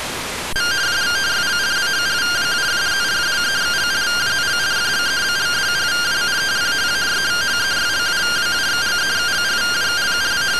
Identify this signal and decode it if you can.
Two-tone teleprinter signal, about 170 Hz shift.